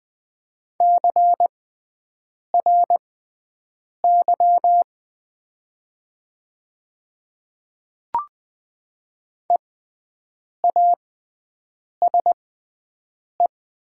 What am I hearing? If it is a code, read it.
Morse code, heard as a keyed tone, about 20 wpm, spaced as for 7 wpm, 700 Hz. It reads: CRY EASE
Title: 6x Character Spacing - Top 500 Words - 20wpm